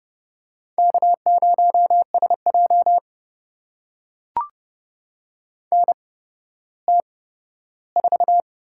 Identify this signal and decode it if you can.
Morse code, heard as a keyed tone, about 30 wpm, spaced as for 15 wpm, 700 Hz. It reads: K0SJ NT4
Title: USA Call Signs - 4 Characters - 8x Character Spacing - 30wpm